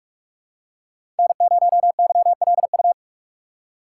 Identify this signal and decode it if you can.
Morse code, heard as a keyed tone, about 44 wpm, 690 Hz. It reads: N0YLU
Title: USA Call Signs - 5 Characters - 6x Character Spacing - 45wpm